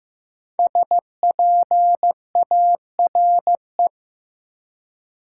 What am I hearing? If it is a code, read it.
Morse code, heard as a keyed tone, about 15 wpm, 700 Hz. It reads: SPARE
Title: Five Letter Words - 15wpm